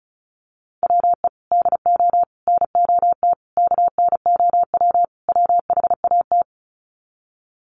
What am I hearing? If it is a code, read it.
Morse code, heard as a keyed tone, about 34 wpm, 700 Hz. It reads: WE DO NOT KNOW WHAT
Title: Sets of 5 Words 35wpm